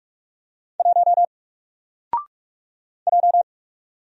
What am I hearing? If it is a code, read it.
Morse code, heard as a keyed tone, about 44 wpm, 700 Hz. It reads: 1 J
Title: Single Letter-Number Alt 45wpm